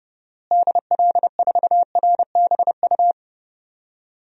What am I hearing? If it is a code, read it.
Morse code, heard as a keyed tone, about 31 wpm, 700 Hz. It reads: DL4RBU
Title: DXCC-100 Call Signs - 6 Characters - Repeated 2x - 30wpm